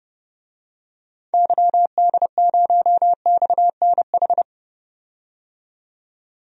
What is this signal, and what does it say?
Morse code, heard as a keyed tone, about 31 wpm, 700 Hz. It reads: YD0XNH
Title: DXCC-100 Call Signs - 6 Characters - 30wpm